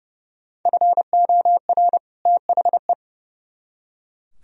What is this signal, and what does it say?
Morse code, heard as a keyed tone, about 36 wpm, 700 Hz. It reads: FOR THE
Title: Speed-Racing - Sets of 2 Words 30wpm